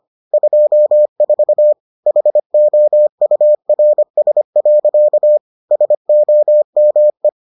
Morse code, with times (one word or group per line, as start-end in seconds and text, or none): 0.3-1.7 24
2.1-5.4 HOURS.
5.7-7.3 SOME